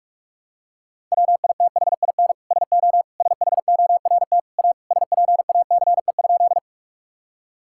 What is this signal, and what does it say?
Morse code, heard as a keyed tone, about 46 wpm, 700 Hz. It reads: WITHIN SO SHORT A SPACE?
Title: Sentences from Top 1000 Words - Review Part 5 - 45wpm